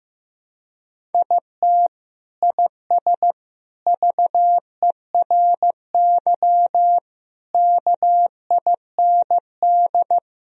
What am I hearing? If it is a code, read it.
Morse code, heard as a keyed tone, about 16 wpm, 700 Hz. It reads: IT IS VERY KIND
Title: Sentences from Top 1000 Words - Review Part 1 - 15wpm